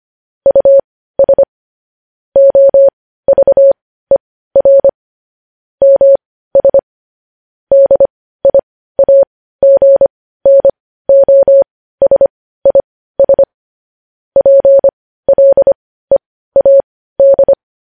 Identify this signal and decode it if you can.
Morse code, heard as a keyed tone, about 25 wpm, spaced as for 15 wpm, 550 Hz. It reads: US OVER MS DIAGNOSIS PLEAD
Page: news headlines in morse code at 15 wpm